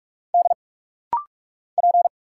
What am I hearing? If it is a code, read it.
Morse code, heard as a keyed tone, about 45 wpm, 700 Hz. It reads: D P
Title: Single Letter-Number Alt - Rapid-Fire 45wpm